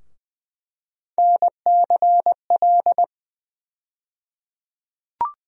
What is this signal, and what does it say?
Morse code, heard as a keyed tone, about 20 wpm, 700 Hz. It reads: NCL
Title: USA Call Sign Postfixes - 3 Letters 20wpm